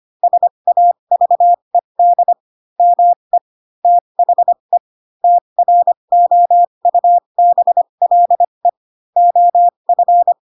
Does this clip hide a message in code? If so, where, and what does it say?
Yes: Morse code from 0.2 to 10.3 s, heard as a keyed tone, about 25 wpm, spaced as for 22 wpm, 700 Hz. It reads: SAVED ME THE TROUBLE OF